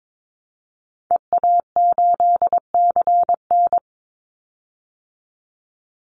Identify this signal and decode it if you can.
Morse code, heard as a keyed tone, about 22 wpm, 690 Hz. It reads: EA8CN
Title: Call Signs Repeated 2x - 22wpm